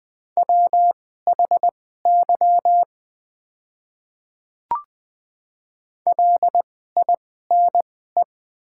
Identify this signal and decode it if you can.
Morse code, heard as a keyed tone, about 20 wpm, spaced as for 14 wpm, 700 Hz. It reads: WHY LINE
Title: 2x Character Spacing - Top 300 Words - 20wpm